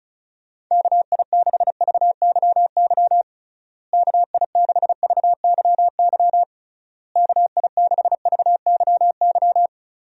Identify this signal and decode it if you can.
Morse code, heard as a keyed tone, about 36 wpm, 700 Hz. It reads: KI6VYY KI6VYY KI6VYY
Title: USA Call Signs - 6 Characters - Repeated 3x - 35wpm